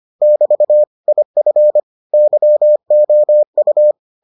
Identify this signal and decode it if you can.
Morse code, heard as a keyed tone, about 25 wpm, 600 Hz. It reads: <BT> IF YOU